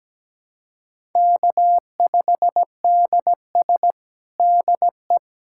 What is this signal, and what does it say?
Morse code, heard as a keyed tone, about 17 wpm, 700 Hz. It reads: K5DS DE
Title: USA Callsign DE Callsign - 4 Characters - 17wpm